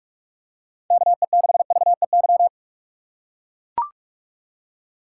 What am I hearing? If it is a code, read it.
Morse code, heard as a keyed tone, about 42 wpm, 700 Hz. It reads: KE6VEY
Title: USA Call Signs - 6 Characters - 4x Character Spacing - 45wpm